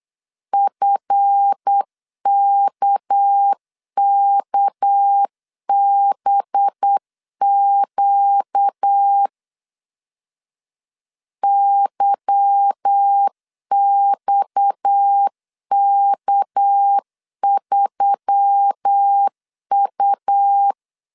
Morse code, heard as a keyed tone, about 8 wpm, 800 Hz. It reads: FKKBQ YXK3U
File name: CW / DET / 1 9 1500C S C 8wpm 5653